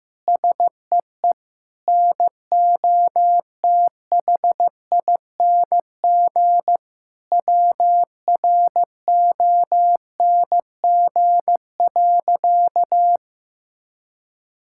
Morse code, heard as a keyed tone, about 15 wpm, 700 Hz. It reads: SEE NOTHING WRONG.
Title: Sentences from Top 1000 Words - Review Part 6 - 15wpm